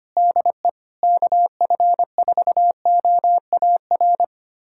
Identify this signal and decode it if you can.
Morse code, heard as a keyed tone, about 25 wpm, 700 Hz. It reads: DE KF4OAR
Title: USA Callsign DE Callsign - 6 Characters - 25wpm